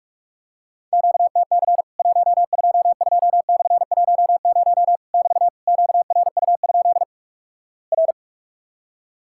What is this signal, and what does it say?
Morse code, heard as a keyed tone, about 48 wpm, 700 Hz. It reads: QTC 122/10 <BT> QRU? R
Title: Worked All Europe DX Contest - QTC Exchange - 45wpm